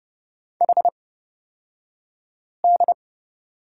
Morse code, heard as a keyed tone, about 30 wpm, spaced as for 5 wpm, 700 Hz. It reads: HD
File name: Lesson 37d - Letter x - Two Characters 30wpm